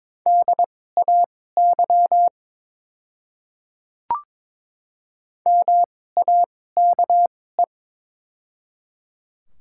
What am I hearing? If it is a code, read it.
Morse code, heard as a keyed tone, about 22 wpm, spaced as for 16 wpm, 690 Hz. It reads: DAY MAKE
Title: 2x Character Spacing - Top 100 Words - 22wpm